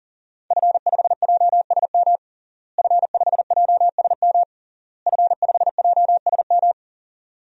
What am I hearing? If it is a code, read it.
Morse code, heard as a keyed tone, about 37 wpm, 700 Hz. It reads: F5JSM F5JSM F5JSM
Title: DXCC-100 Call Signs - 5 Characters - Repeated 3x - 40wpm